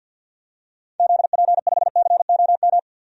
Morse code, heard as a keyed tone, about 53 wpm, 700 Hz. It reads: ZP5COM
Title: DXCC-100 Call Signs - 6 Characters - 50wpm